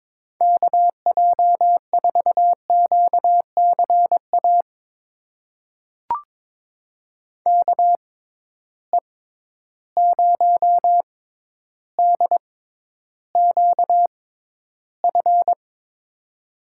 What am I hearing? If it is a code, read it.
Morse code, heard as a keyed tone, about 22 wpm, spaced as for 11 wpm, 700 Hz. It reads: KJ4QCA KE0DQF
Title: USA Call Signs - 6 Characters - 6x Character Spacing - 22wpm